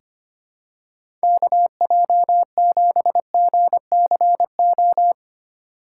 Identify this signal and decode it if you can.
Morse code, heard as a keyed tone, about 25 wpm, 700 Hz. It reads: KJ7GCO
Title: USA Call Signs - 6 Characters - 6x Character Spacing - 25wpm